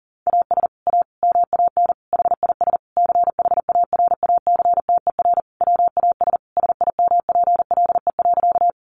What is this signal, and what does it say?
Morse code, heard as a keyed tone, about 40 wpm, 700 Hz. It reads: AS A MAN HIS CHARACTER WAS SIMPLE.
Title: Sentences from Top 1200 Words 40wpm